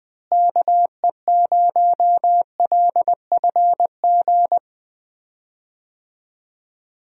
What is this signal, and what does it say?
Morse code, heard as a keyed tone, about 20 wpm, 700 Hz. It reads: KE0LFG
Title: USA Call Signs - 6 Characters - Repeated 2x - 20wpm